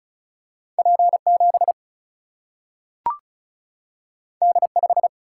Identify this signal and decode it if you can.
Morse code, heard as a keyed tone, about 36 wpm, 700 Hz. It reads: P 7 D 5
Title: Single-Letter Single-Number 35wpm